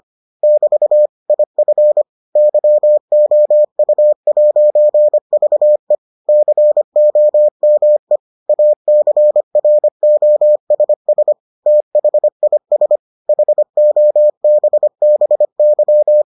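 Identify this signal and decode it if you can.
Morse code, heard as a keyed tone, about 25 wpm, 600 Hz. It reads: <BT> IF YOU'VE COME ACROSS THIS HOBBY
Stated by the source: Morse edition of the Foundations of Amateur Radio episodes.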